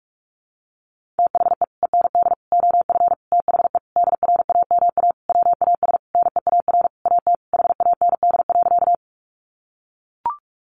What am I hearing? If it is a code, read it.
Morse code, heard as a keyed tone, about 44 wpm, 700 Hz. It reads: THE END OF THE DRAMA WAS NEAR AT HAND.
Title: Sentences from Top 1500 Words 45wpm